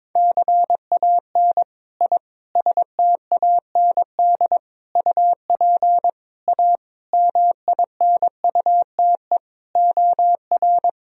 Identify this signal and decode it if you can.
Morse code, heard as a keyed tone, about 22 wpm, 700 Hz. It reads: CAN I STAND UP A MINUTE OR
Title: Sentences from Top 1000 Words - Review Part 1 - 22wpm